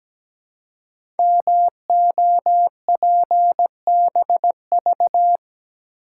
Morse code, heard as a keyed tone, about 17 wpm, 700 Hz. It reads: MOPBV
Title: Code Groups - 5 Letters - 17wpm